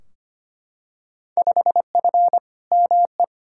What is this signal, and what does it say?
Morse code, heard as a keyed tone, about 25 wpm, 700 Hz. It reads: 5F ME
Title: Field Day Exchanges Encore - 25wpm